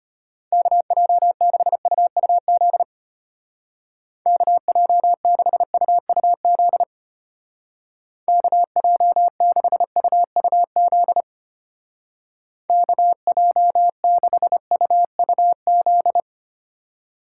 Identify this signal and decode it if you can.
Morse code, heard as a keyed tone, about 32 wpm, 700 Hz. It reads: KJ6UUZ KJ6UUZ KJ6UUZ KJ6UUZ